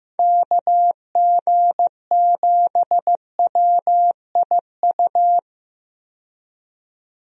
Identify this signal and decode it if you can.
Morse code, heard as a keyed tone, about 15 wpm, 700 Hz. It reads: KG7WIU
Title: USA Call Signs - 6 Characters - 15wpm